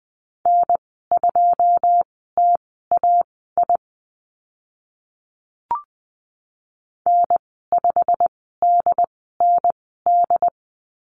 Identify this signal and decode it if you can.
Morse code, heard as a keyed tone, about 20 wpm, spaced as for 15 wpm, 700 Hz. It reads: N2TAI N5DND
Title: 088 Lesson 15g - Number 5 - Call Signs 20wpm